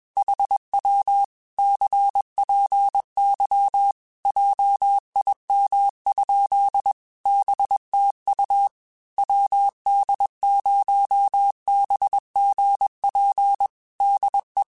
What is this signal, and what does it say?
Morse code, heard as a keyed tone, about 21 wpm, 800 Hz. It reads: HW CPY JIM? BTU WD0BGP DE
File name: QSOs / QSOs-mp3s / WD0BGP 1@20